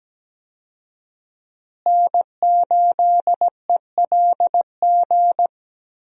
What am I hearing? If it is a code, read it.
Morse code, heard as a keyed tone, about 17 wpm, 700 Hz. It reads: N8ELG
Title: USA Call Signs - 5 Characters - Repeated 3x - 17wpm